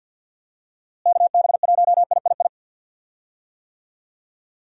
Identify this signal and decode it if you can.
Morse code, heard as a keyed tone, about 48 wpm, 700 Hz. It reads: KB1III
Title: USA Call Signs - 6 Characters - Repeated 3x - 50wpm